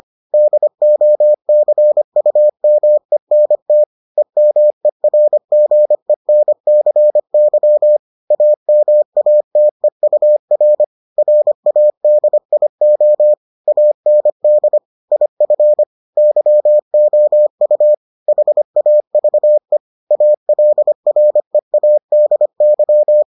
Morse code, heard as a keyed tone, about 25 wpm, 600 Hz. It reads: DOCUMENT EMERGENCY AMATEUR RADIO AND IF YOU HAVE ALREADY